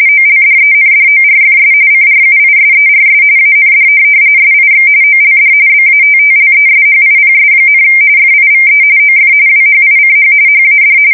RTTY, about 45.45 baud. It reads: OLDENDORFS CRUISERS AND SIX OLD BATTLESHIPS, FIVE OF WHICH HAD
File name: fsk170 / USS-New-Jersey